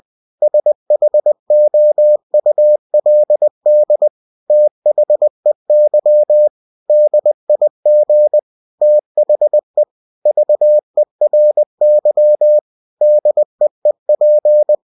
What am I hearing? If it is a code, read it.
Morse code, heard as a keyed tone, about 20 wpm, 600 Hz. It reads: SHOULD THEY DIG THE VERY DEEP
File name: cw-books / winnie-the-pooh / winnie-20wpm-page-0035